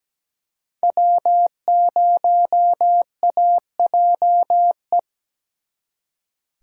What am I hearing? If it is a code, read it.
Morse code, heard as a keyed tone, about 20 wpm, 700 Hz. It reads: W0AJE
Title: Speed Racing - USA 1x3 Call Signs 17wpm